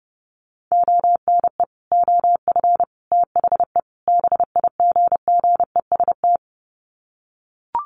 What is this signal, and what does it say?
Morse code, heard as a keyed tone, about 28 wpm, 700 Hz. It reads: ONE OF THE BIGGEST
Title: Sets of 4 Words 30wpm